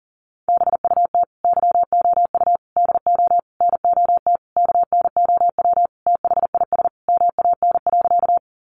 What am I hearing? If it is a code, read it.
Morse code, heard as a keyed tone, about 37 wpm, 700 Hz. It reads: BUT YOU DO NOT KNOW THIS MAN.
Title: Sentences from Top 600 Words 40wpm